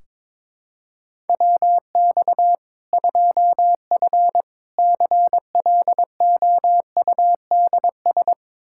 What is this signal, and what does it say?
Morse code, heard as a keyed tone, about 22 wpm, 700 Hz. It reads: WX 2F CLOUDS
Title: Practice QSO - Exchange 2 - Info - 22wpm